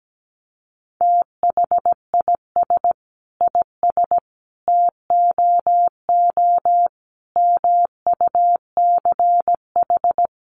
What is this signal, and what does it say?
Morse code, heard as a keyed tone, about 17 wpm, 700 Hz. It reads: THIS IS TOO MUCH
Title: A Sound Of Thunder 17wpm